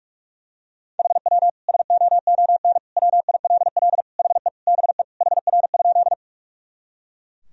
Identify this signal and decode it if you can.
Morse code, heard as a keyed tone, about 41 wpm, 700 Hz. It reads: HW SOON WILL HE BE HR?
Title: Sentences from Top 400 Words and Abbreviations 45wpm